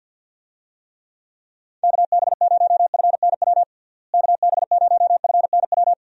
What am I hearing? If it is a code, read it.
Morse code, heard as a keyed tone, about 49 wpm, 700 Hz. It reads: KB0FNW KB0FNW
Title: USA Call Signs - 6 Characters - Repeated 2x - 50wpm